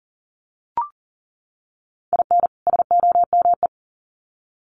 Morse code, heard as a keyed tone, about 40 wpm, 700 Hz. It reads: IN SOME
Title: Sets of 2 Words 40wpm